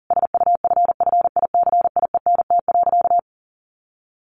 Morse code, heard as a keyed tone, about 40 wpm, 700 Hz. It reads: SUFFICIENT.
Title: Sentences from Top 1400 Words 40wpm